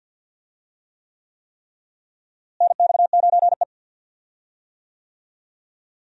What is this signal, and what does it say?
Morse code, heard as a keyed tone, about 52 wpm, 700 Hz. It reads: NX9E
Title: USA Call Signs - 4 Characters - Repeated 3x - 50wpm